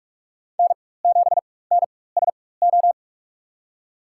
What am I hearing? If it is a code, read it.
Morse code, heard as a keyed tone, about 46 wpm, spaced as for 21 wpm, 700 Hz. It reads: N7NSO
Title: USA Call Signs - 5 Characters - 4x Character Spacing - 45wpm